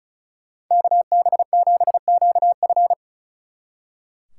Morse code, heard as a keyed tone, about 45 wpm, 700 Hz. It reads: KB7QF